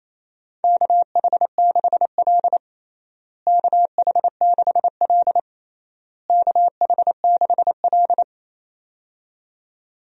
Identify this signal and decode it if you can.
Morse code, heard as a keyed tone, about 28 wpm, 700 Hz. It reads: KH6L KH6L KH6L
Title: DXCC-100 Call Signs - 4 Characters - Repeated 3x - 28wpm